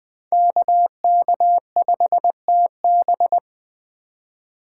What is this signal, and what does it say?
Morse code, heard as a keyed tone, about 20 wpm, 700 Hz. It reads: KK5TB
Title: Speed Racing - USA Call Signs 17wpm